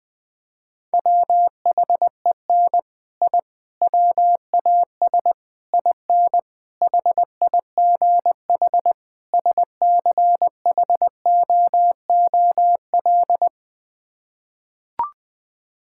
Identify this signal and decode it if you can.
Morse code, heard as a keyed tone, about 20 wpm, 700 Hz. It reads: WHEN I WAS IN HIGH SCHOOL
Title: Sets of 6 Words - 20wpm